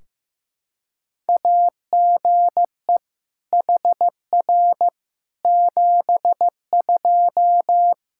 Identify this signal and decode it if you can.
Morse code, heard as a keyed tone, about 16 wpm, 700 Hz. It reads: AGE HR 72
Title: K1USN SST QSO Practice 15wpm